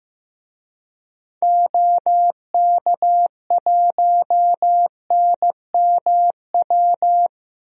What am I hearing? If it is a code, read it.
Morse code, heard as a keyed tone, about 15 wpm, 700 Hz. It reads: OK1NMW
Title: DXCC-100 Call Signs - 6 Characters - Repeated 3x - 15wpm